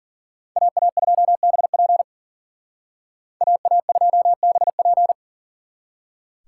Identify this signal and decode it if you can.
Morse code, heard as a keyed tone, about 52 wpm, 700 Hz. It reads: AA2BP AA2BP